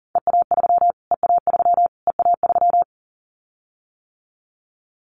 Morse code, heard as a keyed tone, about 40 wpm, 700 Hz. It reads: EA3 EA3 EA3
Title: Callsign Prefixes for Top 25 DXCC - Repeated 3x - 40wpm